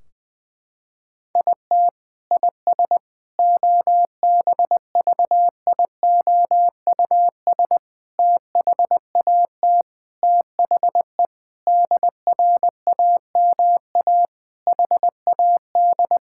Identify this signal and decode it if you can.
Morse code, heard as a keyed tone, about 21 wpm, 700 Hz. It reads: IT IS OBVIOUS THAT THE DRAMA HAD
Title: Sentences from Top 1600 Words 20wpm